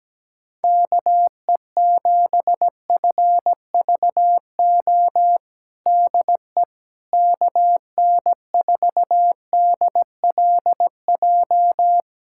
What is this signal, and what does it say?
Morse code, heard as a keyed tone, about 17 wpm, 700 Hz. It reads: KE7FVO DE KN4DLJ
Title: USA Callsign DE Callsign - 6 Characters - 17wpm